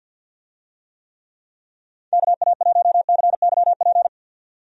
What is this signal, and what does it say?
Morse code, heard as a keyed tone, about 50 wpm, 700 Hz. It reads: KA1CYP
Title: USA Call Signs - 6 Characters - 50wpm